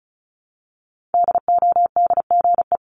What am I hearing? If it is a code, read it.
Morse code, heard as a keyed tone, about 36 wpm, 700 Hz. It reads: DODGE
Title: Multi-letter Phonemes Repeated 3x 35wpm